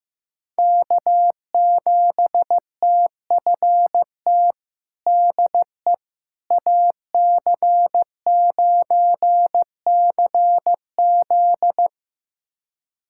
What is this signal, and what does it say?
Morse code, heard as a keyed tone, about 15 wpm, 700 Hz. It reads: K7TFT DE AC9CZ
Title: USA Callsign DE Callsign - 5 Characters - 15wpm